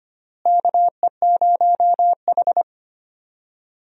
Morse code, heard as a keyed tone, about 25 wpm, 700 Hz. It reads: KE0H
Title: USA Call Signs - 4 Characters - Repeated 2x - 25wpm